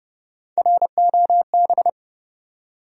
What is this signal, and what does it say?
Morse code, heard as a keyed tone, about 30 wpm, 700 Hz. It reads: ROB
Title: USA Common Names - Max 4 Letters - 30wpm